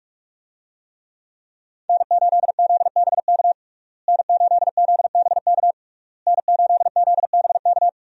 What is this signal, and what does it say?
Morse code, heard as a keyed tone, about 41 wpm, 700 Hz. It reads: N8ZBK N8ZBK N8ZBK
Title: USA Call Signs - 5 Characters - Repeated 3x - 45wpm